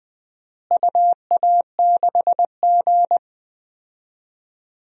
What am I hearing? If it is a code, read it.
Morse code, heard as a keyed tone, about 20 wpm, 700 Hz. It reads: UA6G
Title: DXCC-100 Call Signs - 4 Characters - Repeated 3x - 20wpm